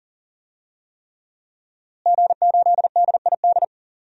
Morse code, heard as a keyed tone, about 40 wpm, 700 Hz. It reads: G8DID
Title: DXCC-100 Call Signs - 5 Characters - Repeated 3x - 40wpm